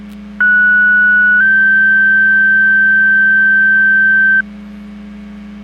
Two-tone page, heard as a one-second tone, then a three-second tone, about 1470 then 1550 Hz. It is heard from 0.4 to 4.4 s.